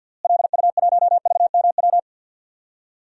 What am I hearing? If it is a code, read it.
Morse code, heard as a keyed tone, about 46 wpm, 700 Hz. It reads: LU1VMW
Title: DXCC-100 Call Signs - 6 Characters - Repeated 3x - 50wpm